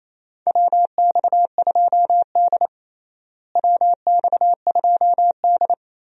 Morse code, heard as a keyed tone, about 28 wpm, 700 Hz. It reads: WX2D WX2D
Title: USA Call Signs - 4 Characters - Repeated 2x - 28wpm